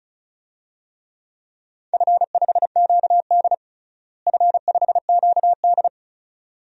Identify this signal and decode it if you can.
Morse code, heard as a keyed tone, about 36 wpm, 700 Hz. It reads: F5QD F5QD
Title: DXCC-100 Call Signs - 4 Characters - Repeated 2x - 35wpm